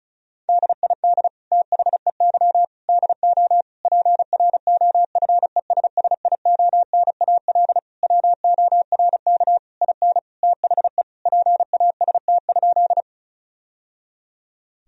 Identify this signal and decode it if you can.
Morse code, heard as a keyed tone, about 36 wpm, 700 Hz. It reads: DID THEY DO PROFESSIONAL WORK IN THE PAST?
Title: Sentences from Top 1000 Words - Review Part 4 - 35wpm